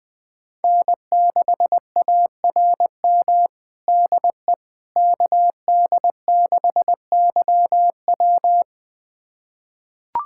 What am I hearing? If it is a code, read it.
Morse code, heard as a keyed tone, about 20 wpm, 700 Hz. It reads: N6ARM DE KD6YW
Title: USA Callsign DE Callsign - 5 Characters - 20wpm